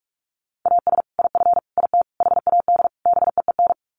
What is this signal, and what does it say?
Morse code, heard as a keyed tone, about 44 wpm, 700 Hz. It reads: AS IF IT HAD BEEN
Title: Sets of 5 Words Encore2 45wpm